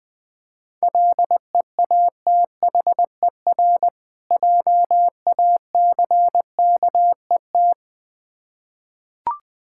Morse code, heard as a keyed tone, about 20 wpm, 700 Hz. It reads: LEATHER JACKET
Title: Difficult Words with Hints - 20wpm